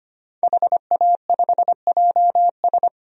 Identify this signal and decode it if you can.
Morse code, heard as a keyed tone, about 25 wpm, 700 Hz. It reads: HA5JS
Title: DXCC-100 Call Signs - 5 Characters - 25wpm